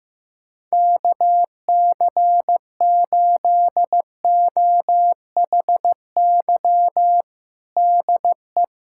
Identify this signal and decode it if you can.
Morse code, heard as a keyed tone, about 16 wpm, 700 Hz. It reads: KC8OHY DE
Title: USA Callsign DE Callsign 15wpm